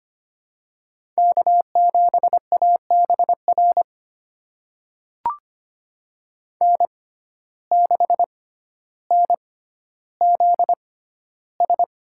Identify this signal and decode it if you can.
Morse code, heard as a keyed tone, about 25 wpm, spaced as for 13 wpm, 700 Hz. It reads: K7ABR N6NZS
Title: USA Call Signs - 5 Characters - 6x Character Spacing - 25wpm